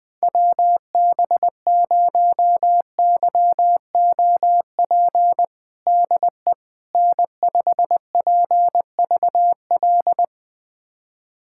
Morse code, heard as a keyed tone, about 20 wpm, 700 Hz. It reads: WB0YOP DE N5PVL
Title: USA Callsign DE Callsign 20wpm